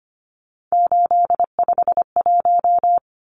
Morse code, heard as a keyed tone, about 25 wpm, 700 Hz. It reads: 851
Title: Numbers - 3-Digit 25wpm